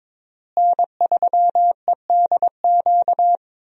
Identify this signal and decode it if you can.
Morse code, heard as a keyed tone, about 22 wpm, 700 Hz. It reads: N3EDQ
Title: USA Call Signs - 5 Characters - Repeated 2x - 22wpm